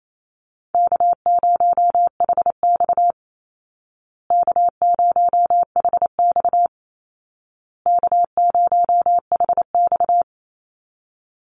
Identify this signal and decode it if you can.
Morse code, heard as a keyed tone, about 28 wpm, 700 Hz. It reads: K0HX K0HX K0HX
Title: Easy Call Signs Repeated 3x - 28wpm